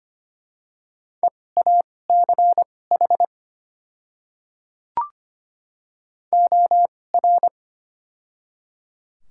Morse code, heard as a keyed tone, about 25 wpm, spaced as for 18 wpm, 700 Hz. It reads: EACH OR
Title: 2x Character Spacing - Top 200 Words - 25wpm